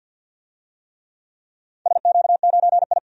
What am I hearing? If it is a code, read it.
Morse code, heard as a keyed tone, about 49 wpm, 700 Hz. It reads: SQ9I
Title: DXCC-100 Call Signs - 4 Characters - 50wpm